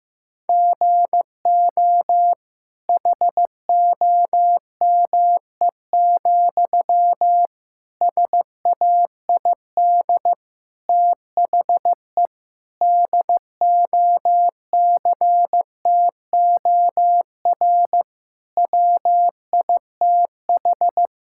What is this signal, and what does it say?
Morse code, heard as a keyed tone, about 15 wpm, 700 Hz. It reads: GO HOME, SAID THE DOCTOR WITH
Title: Sentences from Top 1600 Words 15wpm